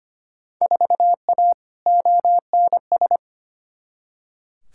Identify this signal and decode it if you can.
Morse code, heard as a keyed tone, about 25 wpm, 700 Hz. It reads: 4A ONS
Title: Field Day Exchanges - 25wpm